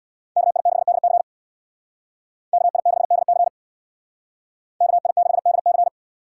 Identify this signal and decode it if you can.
Morse code, heard as a keyed tone, about 66 wpm, 700 Hz. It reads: KI6DC KI6DC KI6DC